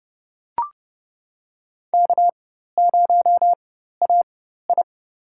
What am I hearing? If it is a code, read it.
Morse code, heard as a keyed tone, about 32 wpm, spaced as for 14 wpm, 700 Hz. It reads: K0AI
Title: USA Call Signs - 4x Character Spacing - 30wpm